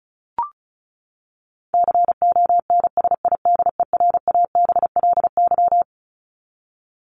Morse code, heard as a keyed tone, about 34 wpm, 700 Hz. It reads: CONSIDERABLY
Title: LY Suffix - Top 100 Words with Base Words 35wpm